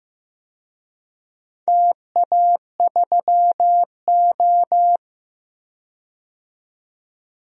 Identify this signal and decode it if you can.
Morse code, heard as a keyed tone, about 15 wpm, 700 Hz. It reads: TA3O
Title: DXCC-100 Call Signs - 4 Characters - 15wpm